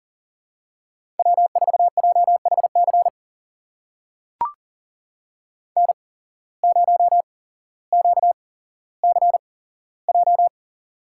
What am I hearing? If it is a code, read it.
Morse code, heard as a keyed tone, about 40 wpm, spaced as for 17 wpm, 700 Hz. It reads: W4JHC N0QCJ
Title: USA Call Signs - 5 Characters - 8x Character Spacing - 40wpm